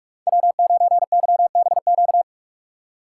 Morse code, heard as a keyed tone, about 46 wpm, 700 Hz. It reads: W9YBQ
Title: USA Call Signs - 5 Characters - 3x Character Spacing - 45wpm